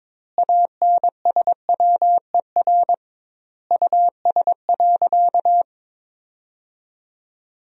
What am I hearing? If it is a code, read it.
Morse code, heard as a keyed tone, about 22 wpm, 700 Hz. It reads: ANSWER US.
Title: 2x Word-Spacing - Sentences from Top 300 Words - 22wpm